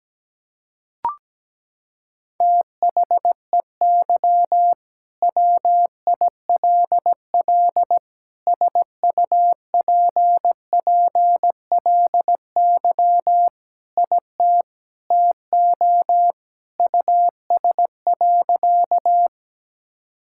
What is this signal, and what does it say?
Morse code, heard as a keyed tone, about 17 wpm, 700 Hz. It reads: THEY WILL SUPPLY IT TO US.
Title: Sentences from Top 1000 Words - Review Part 5 - 17wpm